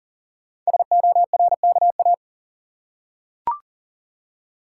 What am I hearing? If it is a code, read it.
Morse code, heard as a keyed tone, about 40 wpm, 690 Hz. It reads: SORKA
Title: Code Groups - 5 Letters - 40wpm